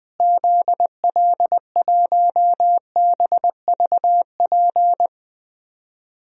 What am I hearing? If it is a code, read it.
Morse code, heard as a keyed tone, about 20 wpm, 700 Hz. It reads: ZL1BVP
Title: DXCC-100 Call Signs - 6 Characters - Repeated 2x - 20wpm